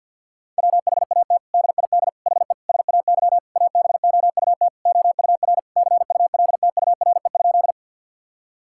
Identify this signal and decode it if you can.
Morse code, heard as a keyed tone, about 52 wpm, 700 Hz. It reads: WHAT DID HE SAY ABOUT OUR CULTURE?
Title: Sentences from Top 1000 Words - Review Part 6 - 50wpm